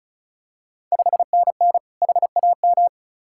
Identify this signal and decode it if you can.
Morse code, heard as a keyed tone, about 36 wpm, 700 Hz. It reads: 5NN HAM
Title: Ontario QSO Party Exchange 35wpm